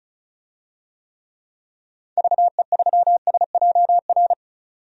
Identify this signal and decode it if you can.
Morse code, heard as a keyed tone, about 34 wpm, 700 Hz. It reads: VE3SJR
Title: DXCC-100 Call Signs - 6 Characters - 35wpm